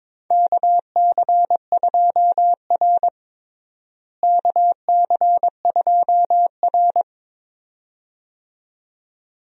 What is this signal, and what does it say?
Morse code, heard as a keyed tone, about 22 wpm, 700 Hz. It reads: KC2R KC2R
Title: USA Call Signs - 4 Characters - Repeated 2x - 22wpm